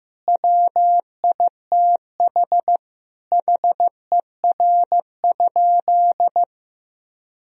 Morse code, heard as a keyed tone, about 15 wpm, 700 Hz. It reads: WITH HER?
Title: Sentences from Top 1000 Words - Review Part 4 - 15wpm